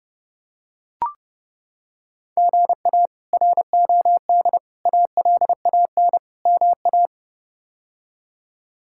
Morse code, heard as a keyed tone, about 30 wpm, 700 Hz. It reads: GA ROD ALAN MA
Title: K1USN SST QSO Practice - 30wpm